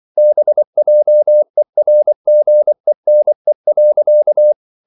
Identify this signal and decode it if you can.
Morse code, heard as a keyed tone, about 24 wpm, 600 Hz. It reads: BJERGENE.
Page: Ferie 24 wpm